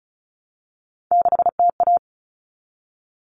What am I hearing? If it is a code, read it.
Morse code, heard as a keyed tone, about 34 wpm, 700 Hz. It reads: 6TA
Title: All Cut Numbers - 3-Digit 35wpm